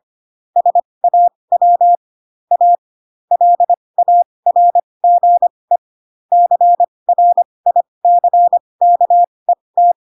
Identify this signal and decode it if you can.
Morse code, heard as a keyed tone, about 25 wpm, spaced as for 20 wpm, 700 Hz. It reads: SAW A LARGE CRICKET
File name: Pinocchio / Pinocchio ch4 15 25 ik1hge